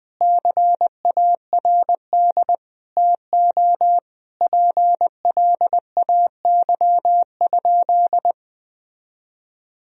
Morse code, heard as a keyed tone, about 20 wpm, 700 Hz. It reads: CARD TO PLAY?
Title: Sentences from Top 1000 Words - Review Part 4 - 20wpm